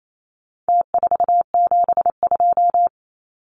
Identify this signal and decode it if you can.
Morse code, heard as a keyed tone, about 28 wpm, 700 Hz. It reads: T472
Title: Numbers - Contest Style - 4-Digit 28wpm